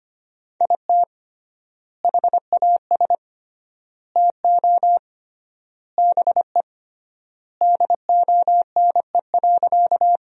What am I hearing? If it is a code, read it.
Morse code, heard as a keyed tone, about 25 wpm, 700 Hz. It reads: IT HAS TO BE DONE.
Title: 3x Word-Spacing - Sentences from Top 500 Words - 25wpm